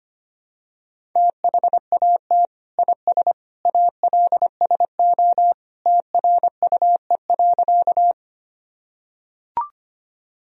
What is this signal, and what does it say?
Morse code, heard as a keyed tone, about 25 wpm, 700 Hz. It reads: THAT IS ALSO TRUE.
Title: Sentences from Top 1000 Words - Review Part 1 - 25wpm